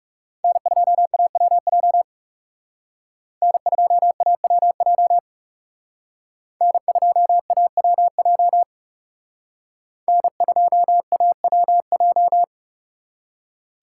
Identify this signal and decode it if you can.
Morse code, heard as a keyed tone, about 36 wpm, 700 Hz. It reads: N2AWJ N2AWJ N2AWJ N2AWJ
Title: Speed Racing - USA 1x3 Call Signs 30wpm